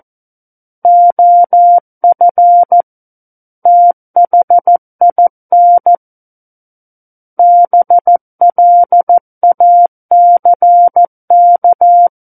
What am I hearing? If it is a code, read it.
Morse code, heard as a keyed tone, about 14 wpm, 700 Hz. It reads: OF THIN BLACK
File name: Book One / 05-Chapter 5 - 14 WPM